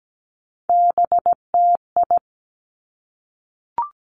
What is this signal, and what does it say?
Morse code, heard as a keyed tone, about 17 wpm, 700 Hz. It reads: BTI
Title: Three Challenging Letter-Number Combinations 17wpm